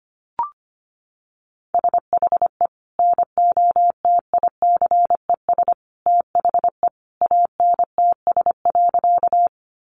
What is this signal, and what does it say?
Morse code, heard as a keyed tone, about 25 wpm, 700 Hz. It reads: SHE NOTICES THE ANTS.
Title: Additional Easy Phrases 25wpm